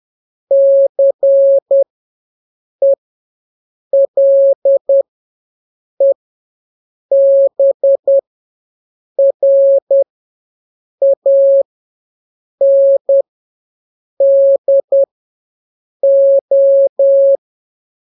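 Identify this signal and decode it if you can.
Morse code, heard as a keyed tone, about 10 wpm, spaced as for 6 wpm, 550 Hz. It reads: CELEBRANDO
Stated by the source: Telegraphy radio challenge for scouts in the Jamboree-On-The-Air.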